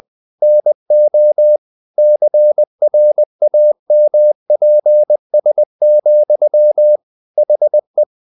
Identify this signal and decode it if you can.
Morse code, heard as a keyed tone, about 20 wpm, 600 Hz. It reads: NO CRAMPS, HE
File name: the old man and the sea part 2 20wpm